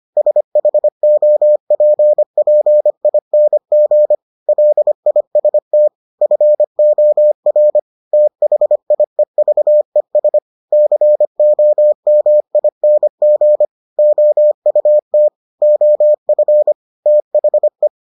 Morse code, heard as a keyed tone, about 25 wpm, 600 Hz. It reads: SHOPPING LIST FOR THIEVES COMING OUT OF THE